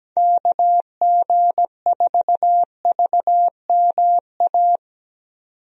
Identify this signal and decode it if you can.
Morse code, heard as a keyed tone, about 17 wpm, 700 Hz. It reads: KG4VMA
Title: USA Call Signs - 6 Characters - 17wpm